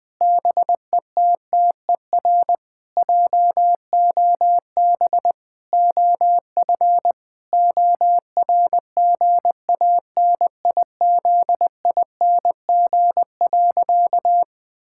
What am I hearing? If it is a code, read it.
Morse code, heard as a keyed tone, about 21 wpm, 700 Hz. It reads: BETTER JOB OF ORGANIZING.
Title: Sentences with 7-Word Phrases - 20wpm